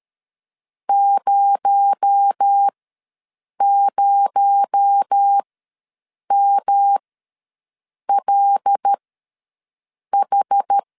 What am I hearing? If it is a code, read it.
Morse code, heard as a keyed tone, about 13 wpm, spaced as for 6 wpm, 800 Hz. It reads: 00MLH
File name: LEZIONI / 6 PKX0 / 6 PKX0 ws 12wpm 3448